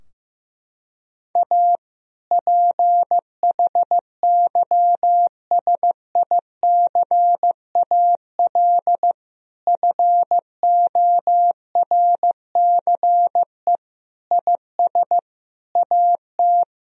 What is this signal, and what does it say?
Morse code, heard as a keyed tone, about 15 wpm, 700 Hz. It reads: A PHYSICAL FORCE IS AT
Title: Sentences from Top 1000 Words - Review Part 4 - 15wpm